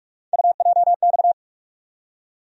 Morse code, heard as a keyed tone, about 41 wpm, 700 Hz. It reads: UJX
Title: USA Call Sign Postfixes - 3 Letters 45wpm